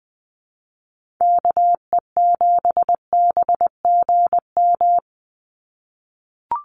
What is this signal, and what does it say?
Morse code, heard as a keyed tone, about 20 wpm, 700 Hz. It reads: KE7BGM
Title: NA SOTA Activator Callsigns 20wpm